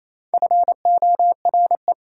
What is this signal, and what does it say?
Morse code, heard as a keyed tone, about 28 wpm, 700 Hz. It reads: FORE
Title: Prefixes and Suffixes 28wpm